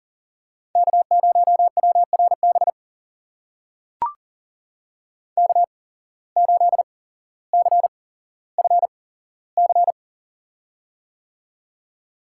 Morse code, heard as a keyed tone, about 40 wpm, spaced as for 17 wpm, 700 Hz. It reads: K0WRB K8CFC
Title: USA Call Signs - 5 Characters - 8x Character Spacing - 40wpm